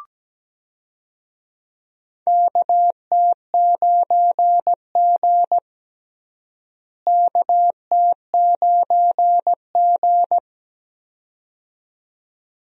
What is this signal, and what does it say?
Morse code, heard as a keyed tone, about 17 wpm, 700 Hz. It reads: KT9G KT9G
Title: USA Call Signs - 4 Characters - Repeated 2x - 17wpm